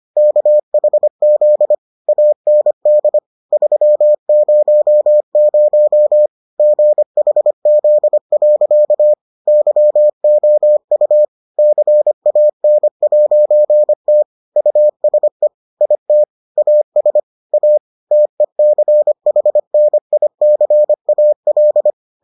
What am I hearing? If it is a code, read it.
Morse code, heard as a keyed tone, about 25 wpm, 600 Hz. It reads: KHZ AND 300 GHZ. YOU CAN'T USE IT AS A TECHNICAL